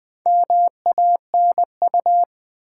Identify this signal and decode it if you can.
Morse code, heard as a keyed tone, about 20 wpm, 700 Hz. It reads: MANU
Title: Build A Word - 4 Letters At A Time - 11-Letter Words - 20wpm